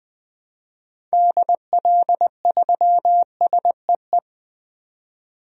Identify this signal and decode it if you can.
Morse code, heard as a keyed tone, about 20 wpm, 700 Hz. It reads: DL3SEE
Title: DXCC-100 Call Signs - 6 Characters - 20wpm